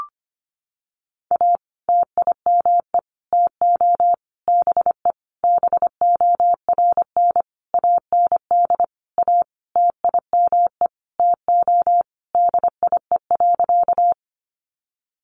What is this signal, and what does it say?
Morse code, heard as a keyed tone, about 25 wpm, 700 Hz. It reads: A TIME TO BE BORN AND A TIME TO DIE.
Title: Sentences from Top 1300 Words 25wpm